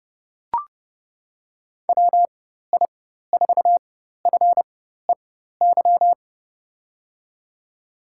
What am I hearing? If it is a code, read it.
Morse code, heard as a keyed tone, about 31 wpm, spaced as for 14 wpm, 700 Hz. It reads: WI4FEY
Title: USA Call Signs - 4x Character Spacing - 30wpm